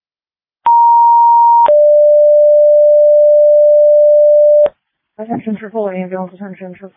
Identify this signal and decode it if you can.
Two-tone page, heard as a one-second tone, then a three-second tone, about 940 then 590 Hz, with speech. It is heard from 0.7 to 4.7 s.
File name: Tripoli Ambulance